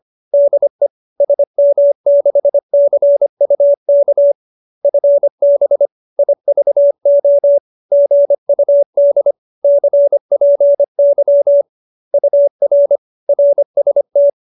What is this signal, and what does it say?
Morse code, heard as a keyed tone, about 25 wpm, 570 Hz. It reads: DE SM6CUK FB IVO GUD CPY UR RST